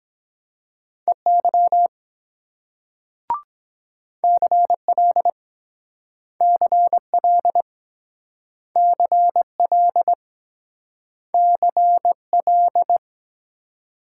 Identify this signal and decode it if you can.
Morse code, heard as a keyed tone, about 23 wpm, 700 Hz. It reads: EY CL CL CL CL